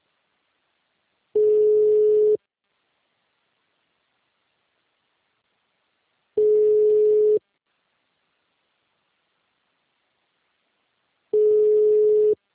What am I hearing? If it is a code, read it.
Ringback tone (Europe, 425 Hz).
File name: out- -2005- - - .388